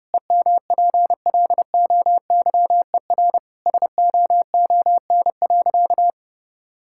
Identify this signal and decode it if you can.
Morse code, heard as a keyed tone, about 32 wpm, 700 Hz. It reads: EMPLOYER SOON.
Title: Sentences from Top 1600 Words 30wpm